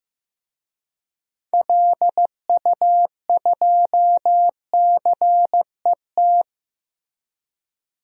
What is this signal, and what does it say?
Morse code, heard as a keyed tone, about 15 wpm, 700 Hz. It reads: LU2CET
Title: DXCC-100 Call Signs - 6 Characters - Repeated 2x - 15wpm